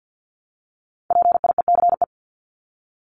Morse code, heard as a keyed tone, about 50 wpm, 700 Hz. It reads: PIECE
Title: Top 500 Words 50wpm